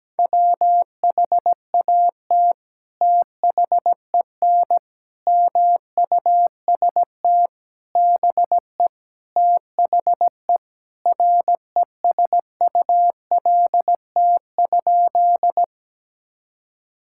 Morse code, heard as a keyed tone, about 17 wpm, 700 Hz. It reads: WHAT THEN MUST BE THE RESULT?
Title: Sentences from Top 1000 Words - Review Part 5 - 17wpm